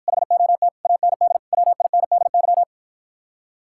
Morse code, heard as a keyed tone, about 53 wpm, 700 Hz. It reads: HOT AND WINDY